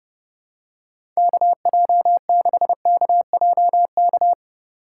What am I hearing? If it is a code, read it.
Morse code, heard as a keyed tone, about 30 wpm, 700 Hz. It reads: KJ6KJK
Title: USA Call Signs - 8x Character Spacing - 30wpm